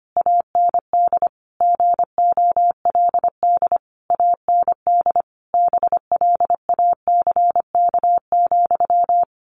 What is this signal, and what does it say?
Morse code, heard as a keyed tone, about 25 wpm, 700 Hz. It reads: AND GOLD AND BLACK,
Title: A Sound Of Thunder 25wpm